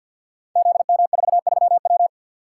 Morse code, heard as a keyed tone, about 52 wpm, 700 Hz. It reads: ZM43W
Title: Code Groups - 5 Letter:Numbers - 50wpm